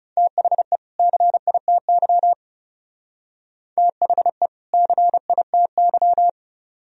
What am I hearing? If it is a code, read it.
Morse code, heard as a keyed tone, about 35 wpm, 700 Hz. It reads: THE CITY THE CITY
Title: Speed-Racing - Sets of 2 Words 30wpm